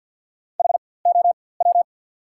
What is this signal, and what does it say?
Morse code, heard as a keyed tone, about 54 wpm, spaced as for 24 wpm, 690 Hz. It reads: HOW
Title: 4x Character Spacing - Top 100 Words - 50wpm